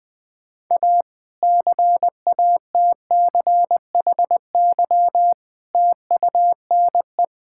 Morse code, heard as a keyed tone, about 20 wpm, 700 Hz. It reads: A CATCHY TUNE
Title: Difficult Sentences with Hints - 20wpm